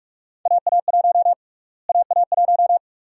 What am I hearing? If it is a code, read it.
Morse code, heard as a keyed tone, about 47 wpm, 700 Hz. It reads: AA1 AA1
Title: USA Call Sign Prefixes - Repeated 2x - 45wpm